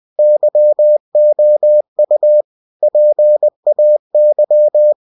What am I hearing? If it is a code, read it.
Morse code, heard as a keyed tone, about 20 wpm, 600 Hz. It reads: YOU PAY
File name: cw-books / winnie-the-pooh / winnie-20wpm-page-0117